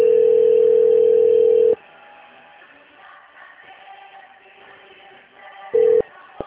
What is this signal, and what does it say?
Ringback tone (North America), 440 + 480 Hz.
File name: out- - - - - .13465